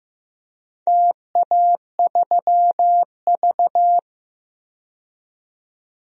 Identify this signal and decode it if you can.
Morse code, heard as a keyed tone, about 15 wpm, 700 Hz. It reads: TA3V
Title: DXCC-100 Call Signs - 4 Characters - 15wpm